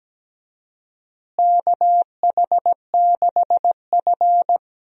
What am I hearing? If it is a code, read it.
Morse code, heard as a keyed tone, about 17 wpm, 700 Hz. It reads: KH6F
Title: USA Call Signs - 4 Characters - Repeated 2x - 17wpm